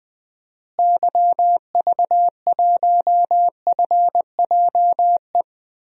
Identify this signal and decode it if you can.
Morse code, heard as a keyed tone, about 20 wpm, 700 Hz. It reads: YV1FJE
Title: DXCC-100 Call Signs - 6 Characters - Repeated 3x - 20wpm